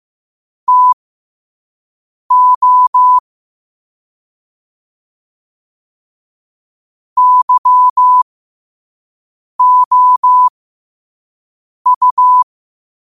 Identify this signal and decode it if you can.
Morse code, heard as a keyed tone, about 15 wpm, spaced as for 5 wpm, 1000 Hz. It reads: TO YOU
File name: qotdfiles / 5wpm